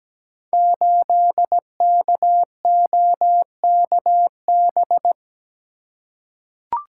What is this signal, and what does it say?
Morse code, heard as a keyed tone, about 17 wpm, 700 Hz. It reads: 8KOKB
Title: Code Groups - 5 Letter:Numbers - 17wpm